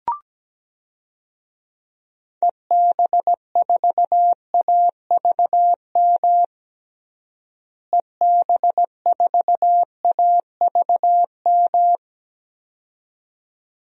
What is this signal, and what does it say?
Morse code, heard as a keyed tone, about 17 wpm, 700 Hz. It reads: EB4AVM EB4AVM
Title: DXCC-100 Call Signs - 6 Characters - Repeated 2x - 17wpm